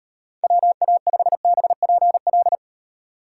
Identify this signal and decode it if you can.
Morse code, heard as a keyed tone, about 38 wpm, 700 Hz. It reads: WA5BPL